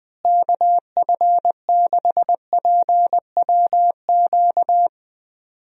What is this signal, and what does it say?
Morse code, heard as a keyed tone, about 20 wpm, 700 Hz. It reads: KF6PWQ
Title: USA Call Signs - 6 Characters - Repeated 3x - 20wpm